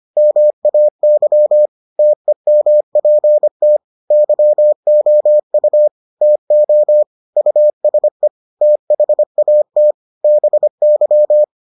Morse code, heard as a keyed tone, about 25 wpm, 600 Hz. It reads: MAY TEMPT YOU TO USE THAT BY